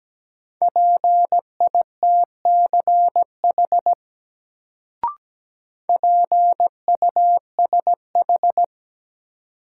Morse code, heard as a keyed tone, about 17 wpm, 700 Hz. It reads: PITCH PUSH
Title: Sentences from Top 1300 Words - New Word Drill - 17wpm